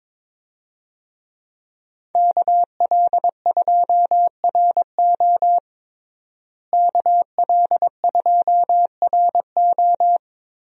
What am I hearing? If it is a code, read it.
Morse code, heard as a keyed tone, about 22 wpm, 700 Hz. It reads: KL2RO KL2RO
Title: USA Call Signs - 5 Characters - Repeated 3x - 22wpm